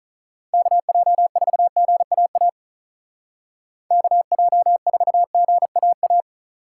Morse code, heard as a keyed tone, about 44 wpm, 700 Hz. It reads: KJ4GAA KJ4GAA